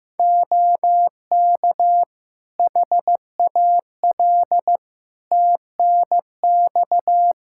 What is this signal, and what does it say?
Morse code, heard as a keyed tone, about 15 wpm, 700 Hz. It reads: OK HAL TNX
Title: Practice QSO - Exchange 4 - Ending - 15wpm